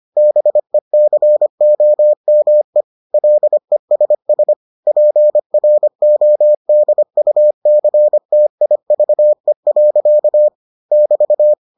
Morse code, heard as a keyed tone, about 25 wpm, 600 Hz. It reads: BECOME LESS PRODUCTIVE. <BT>